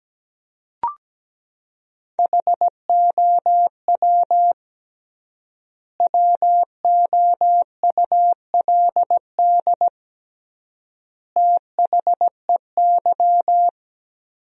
Morse code, heard as a keyed tone, about 17 wpm, 700 Hz. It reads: HOW WOULD THEY
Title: 3x Word-Spacing - Sentences from Top 300 Words - 17wpm